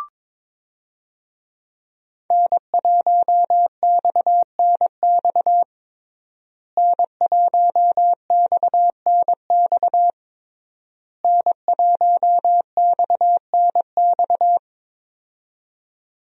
Morse code, heard as a keyed tone, about 22 wpm, 700 Hz. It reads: N1XNX N1XNX N1XNX
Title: USA Call Signs - 5 Characters - Repeated 3x - 22wpm